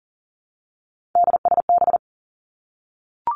Morse code, heard as a keyed tone, about 42 wpm, 700 Hz. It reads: DSB
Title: Three Challenging Letter-Number Combinations 40wpm